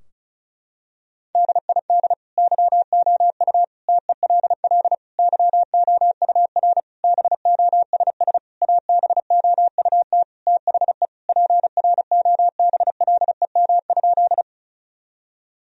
Morse code, heard as a keyed tone, about 34 wpm, 700 Hz. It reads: DID YOU TELL YOUR BOSS ABOUT THE PROBLEM?
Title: Sentences from Top 1000 Words - Review Part 4 - 35wpm